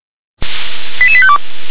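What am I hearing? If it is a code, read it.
Five-tone selective call (ZVEI), heard as quick runs of five tones, tones of about 70 ms. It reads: 88852
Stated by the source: ZVEI five-tone selective call received on SDR.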